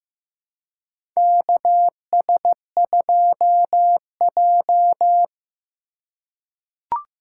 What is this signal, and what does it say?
Morse code, heard as a keyed tone, about 15 wpm, 700 Hz. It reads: KS2J
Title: USA Call Signs - 4 Characters - 3x Character Spacing - 15wpm